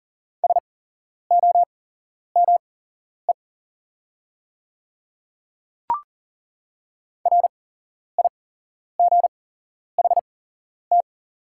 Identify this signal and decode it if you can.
Morse code, heard as a keyed tone, about 42 wpm, spaced as for 11 wpm, 690 Hz. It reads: SOME RIGHT
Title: 8x Character Spacing - Top 300 Words - 40wpm